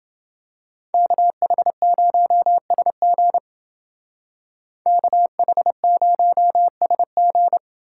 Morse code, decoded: KH0SG KH0SG

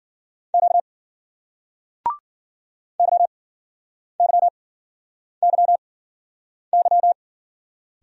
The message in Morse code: Q YYYY

